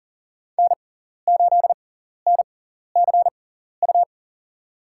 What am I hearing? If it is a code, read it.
Morse code, heard as a keyed tone, about 40 wpm, spaced as for 14 wpm, 700 Hz. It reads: N8NCU
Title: USA Call Signs - 5 Characters - 6x Character Spacing - 40wpm